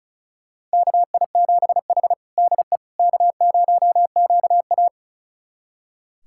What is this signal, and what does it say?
Morse code, heard as a keyed tone, about 36 wpm, 690 Hz. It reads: KI7H DE K0QA
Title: USA Callsign DE Callsign - 4 Characters - 35wpm